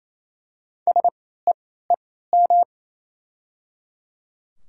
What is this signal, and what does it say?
Morse code, heard as a keyed tone, about 28 wpm, spaced as for 16 wpm, 690 Hz. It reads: SEEM
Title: 3x Character Spacing - Top 500 Words - 28wpm